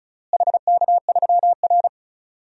Morse code, heard as a keyed tone, about 34 wpm, 700 Hz. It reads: HK3R
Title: DXCC-100 Call Signs - 4 Characters - Repeated 3x - 35wpm